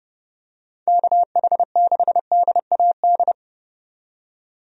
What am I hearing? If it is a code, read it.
Morse code, heard as a keyed tone, about 30 wpm, 700 Hz. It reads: KH6DAD
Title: DXCC-100 Call Signs - 6 Characters - 30wpm